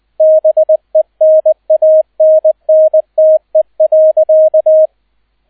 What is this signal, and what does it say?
Morse code, heard as a keyed tone, about 20 wpm, 630 Hz, with radio noise. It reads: BENANNTE.